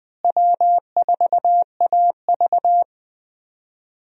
Morse code, heard as a keyed tone, about 21 wpm, 700 Hz. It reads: W4AV
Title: USA Call Signs - 4 Characters - Repeated 3x - 20wpm